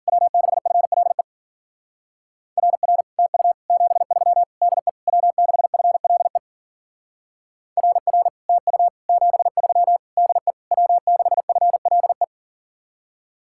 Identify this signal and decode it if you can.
Morse code, heard as a keyed tone, about 49 wpm, 700 Hz. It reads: W6FLE RR TU 73 DE W6FLE RR TU 73 DE W6FLE
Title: Speed-Racing Common QSO Phrases - 40wpm